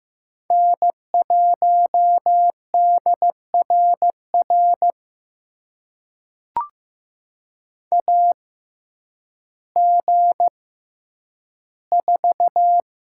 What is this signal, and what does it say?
Morse code, heard as a keyed tone, about 15 wpm, spaced as for 9 wpm, 700 Hz. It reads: N1DRR AG4
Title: USA Call Signs - 5 Characters - 6x Character Spacing - 15wpm